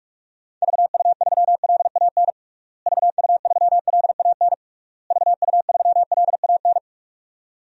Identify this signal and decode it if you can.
Morse code, heard as a keyed tone, about 41 wpm, 700 Hz. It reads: VU3LAN VU3LAN VU3LAN
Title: DXCC-100 Call Signs - 6 Characters - Repeated 3x - 45wpm